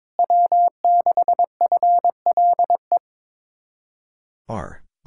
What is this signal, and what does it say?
Morse code, heard as a keyed tone, about 23 wpm, 700 Hz. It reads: W6FLE
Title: Speed-Racing Common QSO Phrases - 22wpm